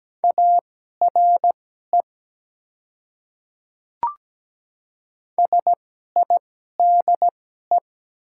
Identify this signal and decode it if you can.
Morse code, heard as a keyed tone, about 17 wpm, spaced as for 12 wpm, 700 Hz. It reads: ARE SIDE
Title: 2x Character Spacing - Top 200 Words - 17wpm